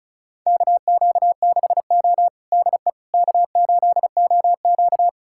Morse code, heard as a keyed tone, about 36 wpm, 700 Hz. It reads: KQ6O DE K8OQ
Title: USA Callsign DE Callsign - 4 Characters - 35wpm